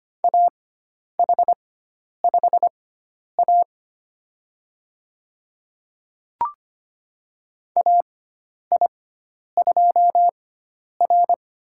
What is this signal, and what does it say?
Morse code, heard as a keyed tone, about 25 wpm, spaced as for 10 wpm, 700 Hz. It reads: AH5A AI2R
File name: Lesson 16g - Letter c - Call Signs 25wpm